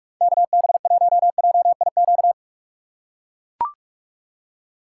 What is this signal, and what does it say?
Morse code, heard as a keyed tone, about 44 wpm, 700 Hz. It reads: KB1JIQ
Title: USA Call Signs - 6 Characters - 3x Character Spacing - 45wpm